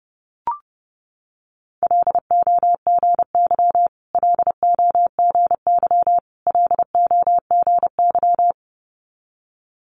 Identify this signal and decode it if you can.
Morse code, heard as a keyed tone, about 30 wpm, 690 Hz. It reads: LOGY LOGY LOGY
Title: Multi-letter Phonemes Repeated 3x 30wpm